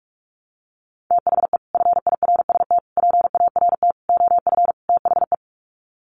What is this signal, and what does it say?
Morse code, heard as a keyed tone, about 44 wpm, 700 Hz. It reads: THE FIRST PART OF THE
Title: Sets of 5 Words Encore 45wpm